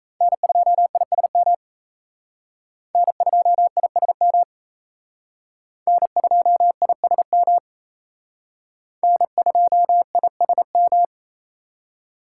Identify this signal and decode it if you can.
Morse code, heard as a keyed tone, about 36 wpm, 700 Hz. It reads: N2ISM N2ISM N2ISM N2ISM